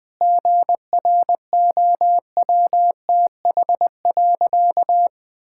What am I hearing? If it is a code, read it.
Morse code, heard as a keyed tone, about 20 wpm, 700 Hz. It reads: GROWTH.
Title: Sentences from Top 1000 Words - Review Part 5 - 20wpm